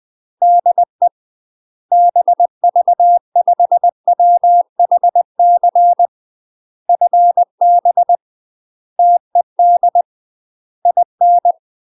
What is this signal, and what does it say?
Morse code, heard as a keyed tone, about 20 wpm, 700 Hz. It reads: DE BV5WHC FB TED IN
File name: UK MASTER QSOs TRAINING / 20WPM / UK MASTER QSOs 0025